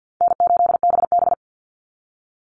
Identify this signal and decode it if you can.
Morse code, heard as a keyed tone, about 48 wpm, 700 Hz. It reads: N8BB
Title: NA SOTA Chaser Callsigns 50wpm